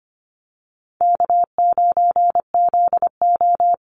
Morse code, heard as a keyed tone, about 25 wpm, 700 Hz. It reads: K9ZO
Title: Easy Call Signs - 25wpm